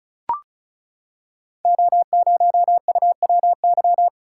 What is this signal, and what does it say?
Morse code, heard as a keyed tone, about 36 wpm, 690 Hz. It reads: O0UWY
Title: Code Groups - 5 Letter:Numbers - 35wpm